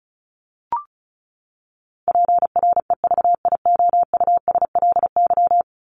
Morse code, heard as a keyed tone, about 34 wpm, 700 Hz. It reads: PREVIOUSLY
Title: LY Suffix - Top 100 Words with Base Words 35wpm